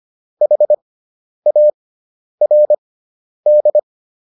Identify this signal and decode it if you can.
Morse code, heard as a keyed tone, about 25 wpm, spaced as for 10 wpm, 600 Hz. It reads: HARD